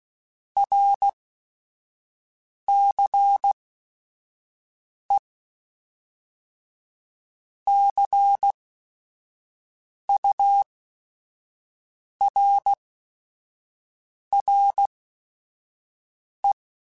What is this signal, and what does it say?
Morse code, heard as a keyed tone, about 16 wpm, spaced as for 5 wpm, 800 Hz. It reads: RCE CURRE
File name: Technical Subjects / Technical Subjects-mp3s / XC @5